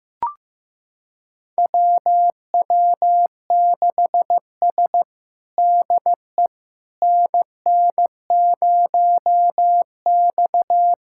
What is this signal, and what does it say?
Morse code, heard as a keyed tone, about 15 wpm, 700 Hz. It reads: WW6S DE NN0X
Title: USA Callsign DE Callsign - 4 Characters - 15wpm